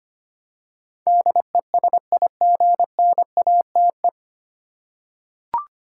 Morse code, heard as a keyed tone, about 25 wpm, 700 Hz. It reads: DESIGNATE
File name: practice / 25wpm / 9 Letter Words - First 3 Letters Given - 25wpm